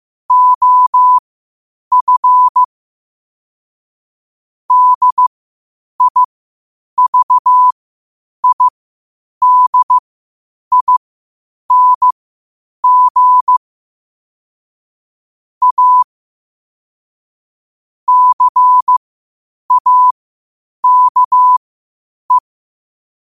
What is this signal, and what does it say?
Morse code, heard as a keyed tone, about 15 wpm, spaced as for 8 wpm, 1000 Hz. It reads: OF DIVIDING A CAKE
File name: qotdfiles / 8wpm